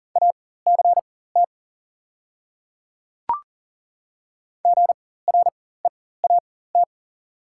Morse code, heard as a keyed tone, about 40 wpm, spaced as for 19 wpm, 690 Hz. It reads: ACT GREAT
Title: 4x Character Spacing - Top 500 Words - 40wpm